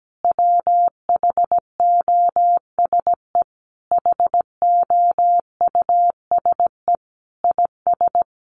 Morse code, heard as a keyed tone, about 17 wpm, 700 Hz. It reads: WHOSE HOUSE IS
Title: Sentences from Top 1200 Words 17wpm